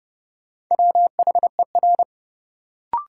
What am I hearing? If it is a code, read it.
Morse code, heard as a keyed tone, about 31 wpm, 700 Hz. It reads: WHER
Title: Sentences from Top 1500 Words - New Word Drill Alt - 30wpm